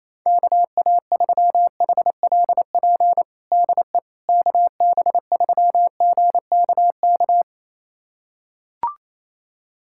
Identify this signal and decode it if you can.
Morse code, heard as a keyed tone, about 27 wpm, 700 Hz. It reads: KA3HLP DE KB3GKK
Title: USA Callsign DE Callsign 28wpm